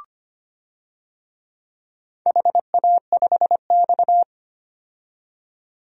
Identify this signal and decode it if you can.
Morse code, heard as a keyed tone, about 25 wpm, 700 Hz. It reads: HA5X
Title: DXCC-100 Call Signs - 4 Characters - 25wpm